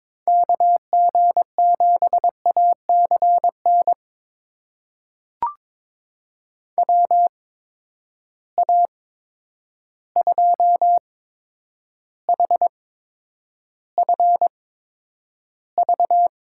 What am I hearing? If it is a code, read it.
Morse code, heard as a keyed tone, about 22 wpm, spaced as for 9 wpm, 700 Hz. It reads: KG7ACN WA2HFV
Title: USA Call Signs - 6 Characters - 8x Character Spacing - 22wpm